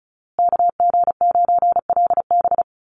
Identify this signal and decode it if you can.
Morse code, heard as a keyed tone, about 34 wpm, 700 Hz. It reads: KG9LB
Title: Call Signs Repeated 2x - 35wpm